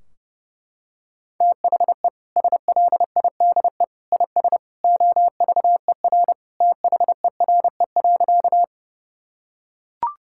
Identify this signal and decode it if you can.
Morse code, heard as a keyed tone, about 30 wpm, 700 Hz. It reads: THE SLIDE IS OVER THERE.
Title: Sentences from Top 1600 Words 30wpm